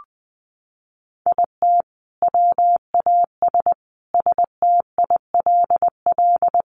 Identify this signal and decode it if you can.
Morse code, heard as a keyed tone, about 19 wpm, 700 Hz. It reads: IT WAS STILL
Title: Sentences from Top 1300 Words 20wpm